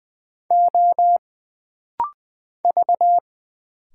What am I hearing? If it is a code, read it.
Morse code, heard as a keyed tone, about 20 wpm, 700 Hz. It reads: O V
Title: Single Letters - Rapid-Fire 20wpm